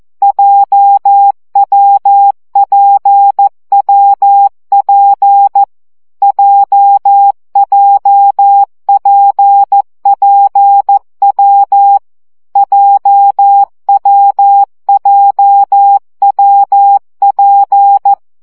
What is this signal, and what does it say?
Morse code, heard as a keyed tone, about 14 wpm, 800 Hz. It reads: JWPWP JJPPW JWJWP
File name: CW gyakorlas w p j 12 baud